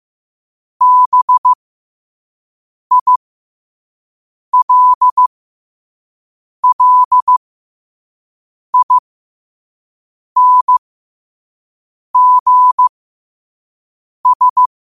Morse code, heard as a keyed tone, about 15 wpm, spaced as for 5 wpm, 1000 Hz. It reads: BILLINGS
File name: qotdfiles / 5wpm